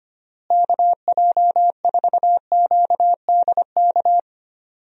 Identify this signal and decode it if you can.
Morse code, heard as a keyed tone, about 25 wpm, 700 Hz. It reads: KJ4QDK
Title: USA Call Signs - 6 Characters - Repeated 3x - 25wpm